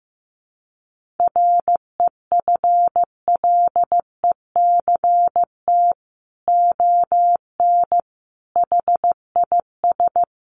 Morse code, heard as a keyed tone, about 15 wpm, 700 Hz. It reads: REFLECT ON HIS
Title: Sentences from Top 1500 Words 15wpm